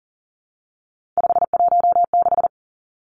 Morse code, heard as a keyed tone, about 39 wpm, 700 Hz. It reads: 516
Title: Three Challenging Letter-Number Combinations 40wpm